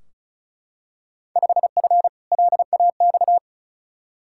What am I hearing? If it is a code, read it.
Morse code, heard as a keyed tone, about 34 wpm, 700 Hz. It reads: 5F LAX
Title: Field Day Exchanges Encore - 35wpm